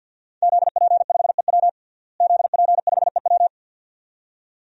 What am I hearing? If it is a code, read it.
Morse code, heard as a keyed tone, about 52 wpm, 700 Hz. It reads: ZP5EW ZP5EW
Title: DXCC-100 Call Signs - 5 Characters - Repeated 2x - 50wpm